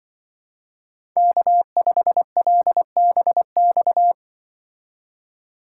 Morse code, heard as a keyed tone, about 23 wpm, 700 Hz. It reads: K5LBX
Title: Speed Racing - USA 1x3 Call Signs 20wpm